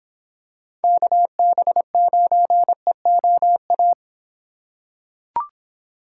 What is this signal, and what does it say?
Morse code, heard as a keyed tone, about 25 wpm, 700 Hz. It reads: KB9EOA